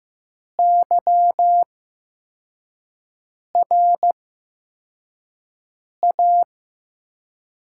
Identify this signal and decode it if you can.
Morse code, heard as a keyed tone, about 15 wpm, spaced as for 4 wpm, 700 Hz. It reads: YRA
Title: USA Call Signs - 6 Characters - 8x Character Spacing - 15wpm